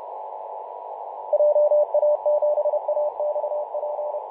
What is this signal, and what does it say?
Morse code, heard as a keyed tone, about 32 wpm, 600 Hz, with radio noise. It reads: JA7AXP
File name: Mar/20 2349z FW5JJ 28.011MHz CW